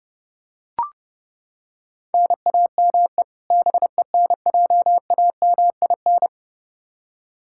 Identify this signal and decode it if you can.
Morse code, heard as a keyed tone, about 32 wpm, 700 Hz. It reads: NAME BENJAMIN
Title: K1USN SST QSO Practice 30wpm